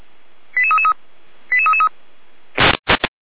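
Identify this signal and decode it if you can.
Five-tone selective call (ZVEI), heard as quick runs of five tones, tones of about 70 ms. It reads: 88333 88333
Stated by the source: ZVEI five-tone selective call received on SDR.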